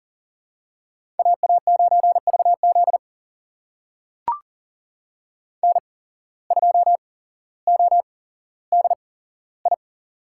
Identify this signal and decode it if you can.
Morse code, heard as a keyed tone, about 40 wpm, spaced as for 17 wpm, 690 Hz. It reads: AA9VZ N2ODI
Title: USA Call Signs - 8x Character Spacing - 40wpm